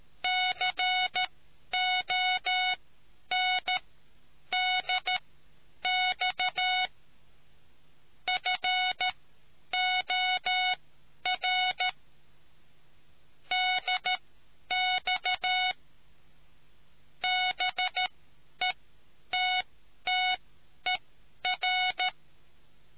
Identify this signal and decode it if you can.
Morse code, heard as a keyed tone, about 13 wpm, spaced as for 9 wpm, 740 Hz. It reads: CONDX FOR DX BETTER